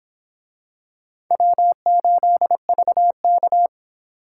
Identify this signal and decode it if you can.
Morse code, heard as a keyed tone, about 24 wpm, 700 Hz. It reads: W8VK